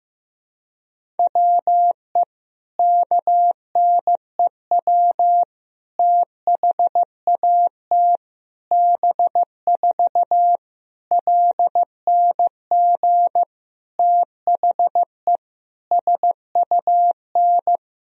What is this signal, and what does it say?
Morse code, heard as a keyed tone, about 15 wpm, 700 Hz. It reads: WE KNEW THAT B4 LNG THE SUN
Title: Sentences from Top 400 Words and Abbreviations 15wpm